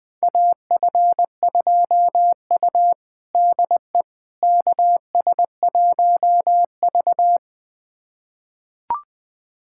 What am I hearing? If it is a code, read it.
Morse code, heard as a keyed tone, about 21 wpm, 700 Hz. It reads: AF2U DE KS1V
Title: USA Callsign DE Callsign - 4 Characters - 20wpm